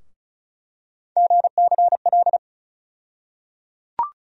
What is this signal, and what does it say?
Morse code, heard as a keyed tone, about 36 wpm, 700 Hz. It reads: GCL
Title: USA Call Sign Postfixes - 3 Letters 35wpm